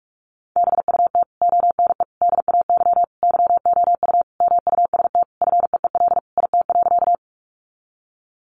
Morse code, heard as a keyed tone, about 46 wpm, 700 Hz. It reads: BUT ONE DAY YOU MUST FEEL IT.
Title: Sentences from Top 400 Words 45wpm